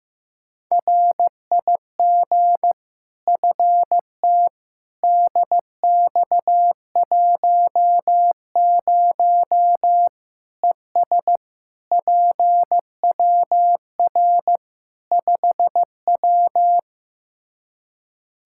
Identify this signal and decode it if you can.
Morse code, heard as a keyed tone, about 15 wpm, 700 Hz. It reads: RIG FT DX10 ES PWR 5W
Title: Practice QSO - Exchange 2 - Info - 15wpm